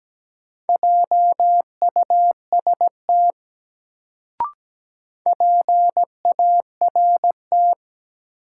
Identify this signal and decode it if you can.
Morse code, heard as a keyed tone, about 17 wpm, 700 Hz. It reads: JUST PART
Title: Sentences from Top 200 Words - New Word Drill - 17wpm